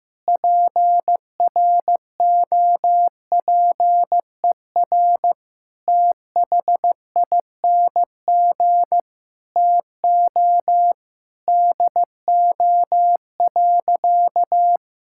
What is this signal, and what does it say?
Morse code, heard as a keyed tone, about 15 wpm, 700 Hz. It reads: PROPER THING TO DO.
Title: Sentences from Top 1000 Words - Review Part 5 - 15wpm